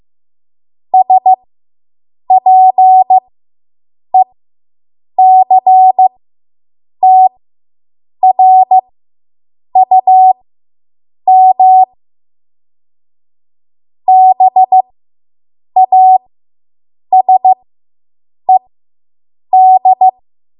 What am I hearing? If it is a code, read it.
Morse code, heard as a keyed tone, about 15 wpm, spaced as for 7 wpm, 750 Hz. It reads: SPECTRUM BASED